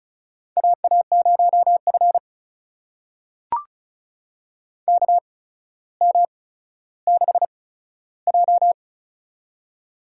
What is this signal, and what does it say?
Morse code, heard as a keyed tone, about 36 wpm, spaced as for 15 wpm, 700 Hz. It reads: AA0F KM6J
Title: USA Call Signs - 4 Characters - 8x Character Spacing - 35wpm